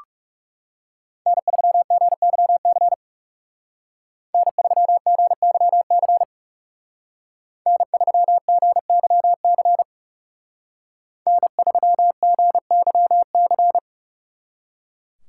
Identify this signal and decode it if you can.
Morse code, heard as a keyed tone, about 39 wpm, 700 Hz. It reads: N3GYC N3GYC N3GYC N3GYC